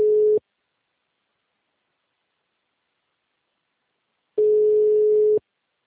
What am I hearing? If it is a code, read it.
Ringback tone (Europe, 425 Hz).